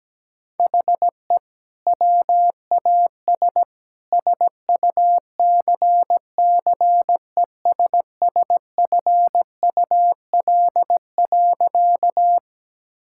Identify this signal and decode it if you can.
Morse code, heard as a keyed tone, about 17 wpm, 700 Hz. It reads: HE WAS SUCCESSFUL.
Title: Sentences from Top 1000 Words - Review Part 6 - 17wpm